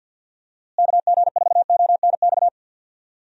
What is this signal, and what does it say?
Morse code, heard as a keyed tone, about 48 wpm, 700 Hz. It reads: KG4ONX
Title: USA Call Signs - 6 Characters - Repeated 2x - 50wpm